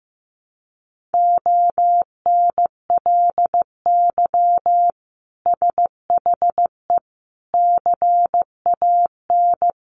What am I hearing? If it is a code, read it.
Morse code, heard as a keyed tone, about 15 wpm, 700 Hz. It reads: ONLY SHE CAN
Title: Sentences from Top 1100 Words 15wpm